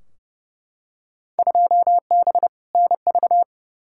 Morse code, heard as a keyed tone, about 31 wpm, 700 Hz. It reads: 2B NV
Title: Field Day Exchanges Encore - 30wpm